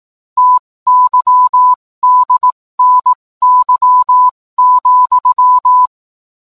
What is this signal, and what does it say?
Morse code, heard as a keyed tone, about 18 wpm, spaced as for 15 wpm, 1000 Hz. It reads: TYDNY,